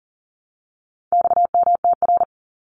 Morse code, heard as a keyed tone, about 40 wpm, 700 Hz. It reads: XMTR
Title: Abbreviations 40wpm